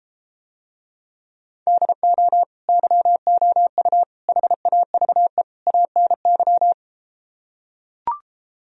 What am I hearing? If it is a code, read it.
Morse code, heard as a keyed tone, about 33 wpm, 700 Hz. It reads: DO YOU HAVE ANY